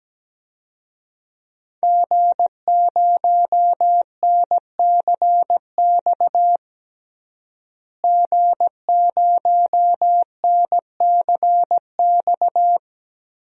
Morse code, heard as a keyed tone, about 17 wpm, 700 Hz. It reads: G0NCX G0NCX
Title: DXCC-100 Call Signs - 5 Characters - Repeated 3x - 17wpm